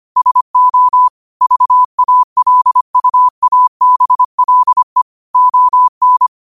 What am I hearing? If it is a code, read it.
Morse code, heard as a keyed tone, about 25 wpm, 1000 Hz. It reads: SO VALUABLE ON
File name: qotdfiles / 25wpm